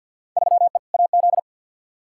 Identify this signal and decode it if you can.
Morse code, heard as a keyed tone, about 48 wpm, 700 Hz. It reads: 3E AZ
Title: Field Day Exchanges - 50wpm